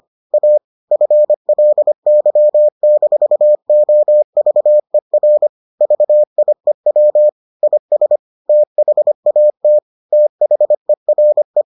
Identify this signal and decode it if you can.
Morse code, heard as a keyed tone, about 25 wpm, 600 Hz. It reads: A FLY-OVER VIEW IS THAT THERE